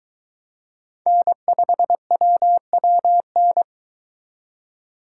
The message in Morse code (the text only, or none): N5WWN